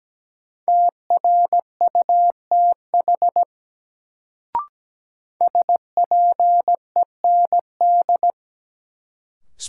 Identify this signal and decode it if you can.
Morse code, heard as a keyed tone, about 17 wpm, 700 Hz. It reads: TRUTH SPEND
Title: Sets of 5 Words - New Word Drill - 17wpm